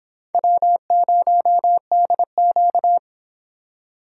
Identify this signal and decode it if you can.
Morse code, heard as a keyed tone, about 26 wpm, 700 Hz. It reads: W0DQ